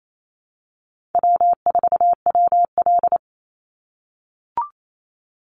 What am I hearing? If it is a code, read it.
Morse code, heard as a keyed tone, about 28 wpm, 700 Hz. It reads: W4WL
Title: USA Call Signs 28wpm